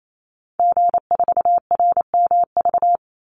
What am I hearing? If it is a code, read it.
Morse code, heard as a keyed tone, about 28 wpm, 700 Hz. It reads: G4RMV
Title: Call Signs Repeated 3x - 28wpm